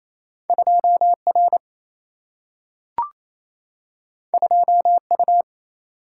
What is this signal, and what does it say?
Morse code, heard as a keyed tone, about 28 wpm, 700 Hz. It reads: 2 R 2 U
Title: Single-Letter Single-Number 28wpm